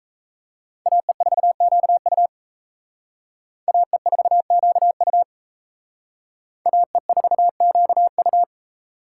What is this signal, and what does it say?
Morse code, heard as a keyed tone, about 34 wpm, 700 Hz. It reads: AE4QU AE4QU AE4QU
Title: Speed Racing - USA Call Signs 28wpm